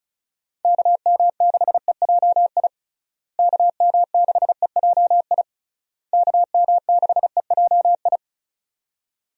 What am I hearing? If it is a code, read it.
Morse code, heard as a keyed tone, about 34 wpm, 700 Hz. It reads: KM6EJI KM6EJI KM6EJI
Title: USA Call Signs - 6 Characters - Repeated 3x - 35wpm